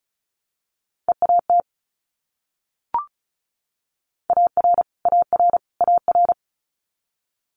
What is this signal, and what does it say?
Morse code, heard as a keyed tone, about 36 wpm, 700 Hz. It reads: EAT AR AR AR
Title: Multi-letter Phonemes Repeated 3x 35wpm